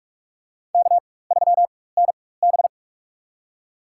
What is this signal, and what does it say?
Morse code, heard as a keyed tone, about 44 wpm, spaced as for 21 wpm, 690 Hz. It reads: K3NB
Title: USA Call Signs - 4 Characters - 4x Character Spacing - 45wpm